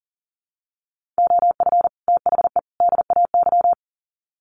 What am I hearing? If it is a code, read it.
Morse code, heard as a keyed tone, about 38 wpm, 700 Hz. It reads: OF THE DAY
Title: Sets of 3 Words 40wpm